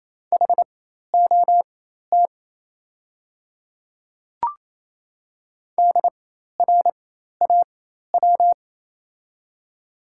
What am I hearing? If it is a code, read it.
Morse code, heard as a keyed tone, about 27 wpm, spaced as for 13 wpm, 700 Hz. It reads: HOT DRAW
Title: 4x Character Spacing - Top 500 Words - 28wpm